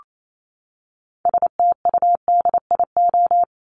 Morse code, heard as a keyed tone, about 28 wpm, 700 Hz. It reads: STUDIO
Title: Top 1001-1250 Words 28wpm